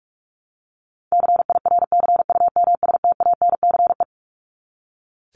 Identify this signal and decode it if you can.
Morse code, heard as a keyed tone, about 44 wpm, 700 Hz. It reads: CIRCUMSTANCE
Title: ES:S Suffix - Top 100 Words with Base Words 45wpm